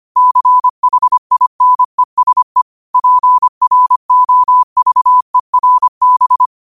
Morse code, heard as a keyed tone, about 25 wpm, 1000 Hz. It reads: CHINESE PROVERB